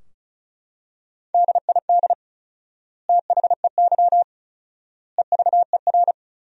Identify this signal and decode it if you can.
Morse code, heard as a keyed tone, about 36 wpm, 700 Hz. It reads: DID THEY EVER
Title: 4x Word-Spacing - Sentences from Top 400 Words - 35wpm